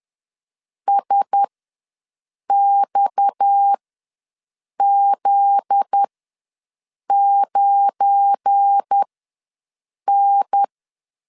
Morse code, heard as a keyed tone, about 10 wpm, spaced as for 6 wpm, 800 Hz. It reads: SXZ9N
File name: LEZIONI / 7 GF9Z / 7 GF9Z ws 10wpm 3919